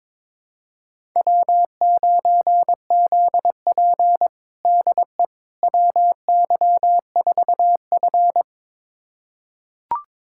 Morse code, heard as a keyed tone, about 22 wpm, 700 Hz. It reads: W9ZP DE WY4F
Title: USA Callsign DE Callsign - 4 Characters - 22wpm